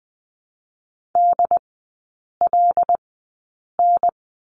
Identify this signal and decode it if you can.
Morse code, heard as a keyed tone, about 20 wpm, spaced as for 8 wpm, 700 Hz. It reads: DLN
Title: 074 Lesson 13e - Letter L - Three Characters 20wpm